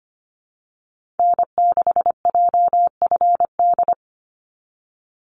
Morse code, heard as a keyed tone, about 25 wpm, 700 Hz. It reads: N6JFD
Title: NA SOTA Activator Callsigns 25wpm